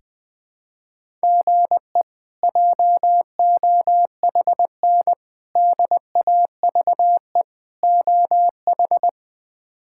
Morse code, heard as a keyed tone, about 19 wpm, 700 Hz. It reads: GE JOHN DAVE OH
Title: K1USN SST QSO Practice - 20wpm